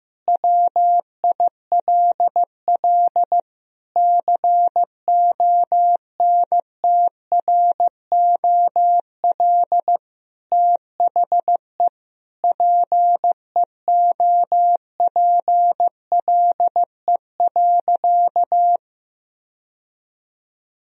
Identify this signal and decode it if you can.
Morse code, heard as a keyed tone, about 16 wpm, 700 Hz. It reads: WILL CONTROL THE PEOPLE.
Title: Sentences from Top 1000 Words - Review Part 3 - 15wpm